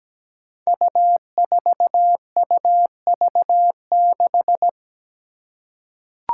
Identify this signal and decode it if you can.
Morse code, heard as a keyed tone, about 17 wpm, 700 Hz. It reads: U4UV6
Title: Five Challenging Letter-Number Combinations 17wpm